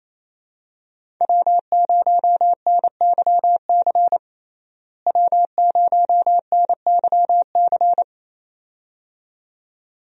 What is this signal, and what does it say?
Morse code, heard as a keyed tone, about 28 wpm, 700 Hz. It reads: W0NYC W0NYC
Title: USA Call Signs - 5 Characters - Repeated 2x - 28wpm